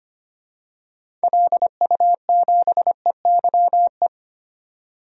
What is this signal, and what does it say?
Morse code, heard as a keyed tone, about 25 wpm, 700 Hz. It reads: LU7EYE
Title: DXCC-100 Call Signs - 6 Characters - 25wpm